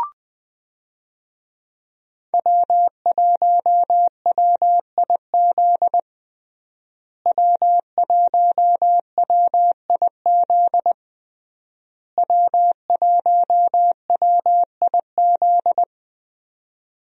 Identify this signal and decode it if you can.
Morse code, heard as a keyed tone, about 20 wpm, 700 Hz. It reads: W1WIZ W1WIZ W1WIZ
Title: USA Call Signs - 5 Characters - Repeated 3x - 20wpm